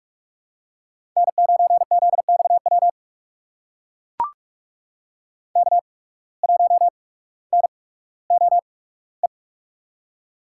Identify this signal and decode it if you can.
Morse code, heard as a keyed tone, about 41 wpm, spaced as for 19 wpm, 700 Hz. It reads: N9ZXW K1NOE
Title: USA Call Signs - 5 Characters - 8x Character Spacing - 45wpm